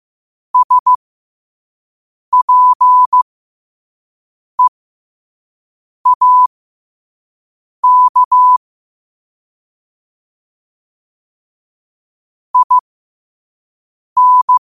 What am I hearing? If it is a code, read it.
Morse code, heard as a keyed tone, about 15 wpm, spaced as for 5 wpm, 1000 Hz. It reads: SPEAK IN
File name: qotdfiles / 5wpm